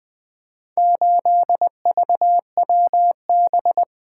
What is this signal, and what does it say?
Morse code, heard as a keyed tone, about 20 wpm, 700 Hz. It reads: 8VWB
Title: Four Letter-Number Combinations - 20wpm